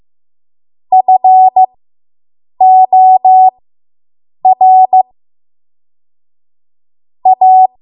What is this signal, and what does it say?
Morse code, heard as a keyed tone, about 15 wpm, spaced as for 7 wpm, 750 Hz. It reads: FOR A